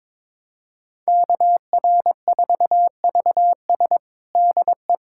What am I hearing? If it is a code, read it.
Morse code, heard as a keyed tone, about 21 wpm, 700 Hz. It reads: KR4VS DE
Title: USA Callsign DE Callsign - 5 Characters - 22wpm